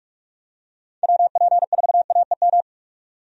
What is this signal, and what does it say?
Morse code, heard as a keyed tone, about 46 wpm, 700 Hz. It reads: WP4AEM
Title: USA Call Signs - 6 Characters - 45wpm